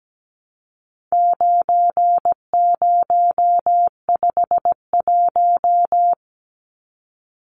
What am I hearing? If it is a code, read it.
Morse code, heard as a keyed tone, about 17 wpm, 700 Hz. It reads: 9051
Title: Numbers - 4-Digit 17wpm